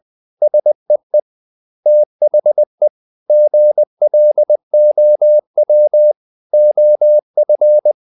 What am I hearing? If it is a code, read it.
Morse code, heard as a keyed tone, about 20 wpm, 600 Hz. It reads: SEE THE GLOW OF
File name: the old man and the sea part 2 20wpm